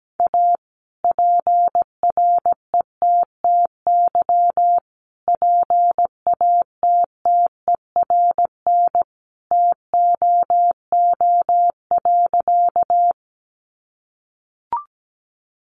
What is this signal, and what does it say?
Morse code, heard as a keyed tone, about 17 wpm, 700 Hz. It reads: A PRETTY PATTERN TOO.
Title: Sentences from Top 1000 Words 17wpm